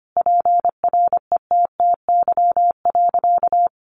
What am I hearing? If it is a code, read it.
Morse code, heard as a keyed tone, about 25 wpm, 700 Hz. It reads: PRETTY.
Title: Easy Phrases 25wpm